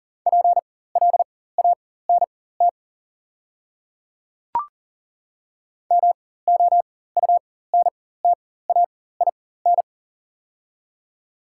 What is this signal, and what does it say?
Morse code, heard as a keyed tone, about 40 wpm, spaced as for 19 wpm, 700 Hz. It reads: PLANT MOUNTAIN
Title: 4x Character Spacing - Top 500 Words - 40wpm